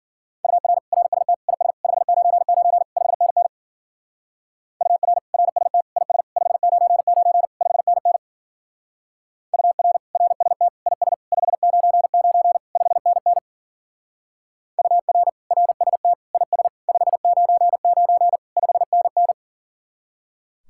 Morse code, heard as a keyed tone, about 47 wpm, 700 Hz. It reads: UR RST IS 599 5NN UR RST IS 599 5NN UR RST IS 599 5NN UR RST IS 599 5NN
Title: Speed-Racing Common QSO Phrases - 40wpm